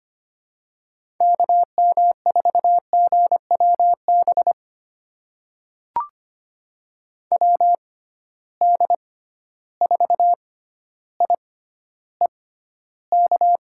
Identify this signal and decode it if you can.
Morse code, heard as a keyed tone, about 25 wpm, spaced as for 13 wpm, 700 Hz. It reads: KM4GWB WD4IEK
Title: USA Call Signs - 6x Character Spacing - 25wpm